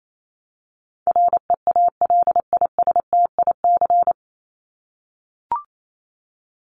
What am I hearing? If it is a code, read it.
Morse code, heard as a keyed tone, about 28 wpm, 700 Hz. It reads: REALISTIC
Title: Top 1751-2000 Words 28wpm